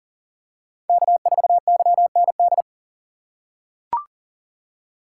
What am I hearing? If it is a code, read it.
Morse code, heard as a keyed tone, about 40 wpm, 700 Hz. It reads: K4YND
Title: K1USN SST QSO Practice - 40wpm